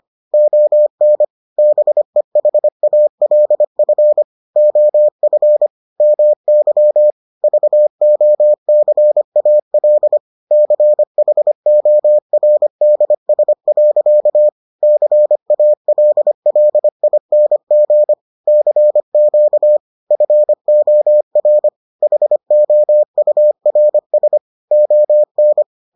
Morse code, heard as a keyed tone, about 25 wpm, 600 Hz. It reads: ON BEHALF OF MY VOCAL CHORDS. CALLING CQ FOR HOURS ON